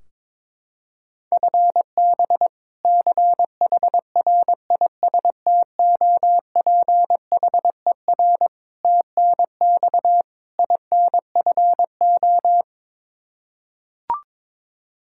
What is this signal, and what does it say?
Morse code, heard as a keyed tone, about 22 wpm, 700 Hz. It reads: FB CHRISTOPHER TNX INFO
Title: K1USN SST QSO Practice 22wpm